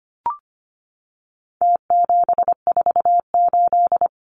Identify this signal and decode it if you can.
Morse code, heard as a keyed tone, about 25 wpm, 700 Hz. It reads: T748
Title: Numbers - Contest Style - 4-Digit 25wpm